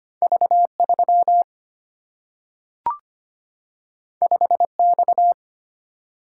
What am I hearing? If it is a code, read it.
Morse code, heard as a keyed tone, about 25 wpm, 700 Hz. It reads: V 3 5 X
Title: Single-Letter Single-Number 25wpm